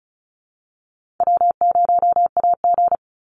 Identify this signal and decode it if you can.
Morse code, heard as a keyed tone, about 34 wpm, 700 Hz. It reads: W0AG
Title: Call Signs - 35wpm